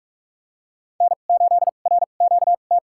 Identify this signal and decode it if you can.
Morse code, heard as a keyed tone, about 45 wpm, spaced as for 32 wpm, 700 Hz. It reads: N8RQT
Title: USA Call Signs - 2x Character Spacing - 45wpm